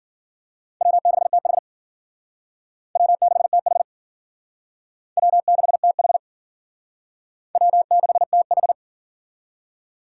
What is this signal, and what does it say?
Morse code, heard as a keyed tone, about 50 wpm, 700 Hz. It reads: W6TH W6TH W6TH W6TH
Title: Speed Racing - USA 1x2 Call Signs 40wpm